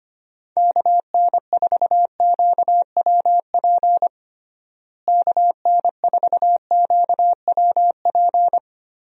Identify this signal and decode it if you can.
Morse code, heard as a keyed tone, about 25 wpm, 700 Hz. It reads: KN4QWP KN4QWP
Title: USA Call Signs - 6 Characters - Repeated 3x - 25wpm